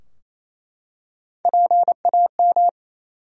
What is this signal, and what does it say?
Morse code, heard as a keyed tone, about 28 wpm, 700 Hz. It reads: PAM
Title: USA Common Names - Max 3 Letters - 28wpm